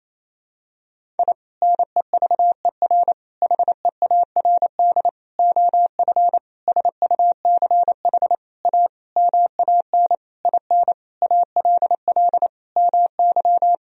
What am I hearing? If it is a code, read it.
Morse code, heard as a keyed tone, about 28 wpm, 700 Hz. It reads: I NEVER HEARD OF SUCH A MAN IN ALL MY
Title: Sentences from Top 1000 Words - Review Part 3 - 28wpm